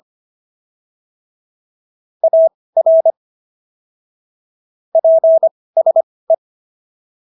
Morse code, heard as a keyed tone, about 25 wpm, spaced as for 18 wpm, 650 Hz. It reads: AR PSE